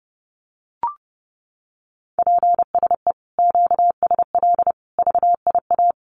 Morse code, heard as a keyed tone, about 30 wpm, 700 Hz. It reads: PSE QSL VIA
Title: Common QSO Phrases Encore 30wpm